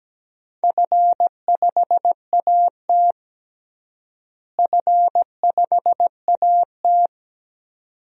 Morse code, heard as a keyed tone, about 17 wpm, 700 Hz. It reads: F5AT F5AT
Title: DXCC-100 Call Signs - 4 Characters - Repeated 2x - 17wpm